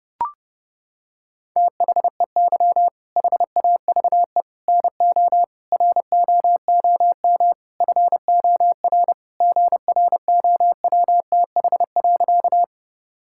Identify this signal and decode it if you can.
Morse code, heard as a keyed tone, about 32 wpm, 700 Hz. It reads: THEY HAVE NO ROOM FOR GROWTH.
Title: Sentences from Top 1000 Words - Review Part 5 - 30wpm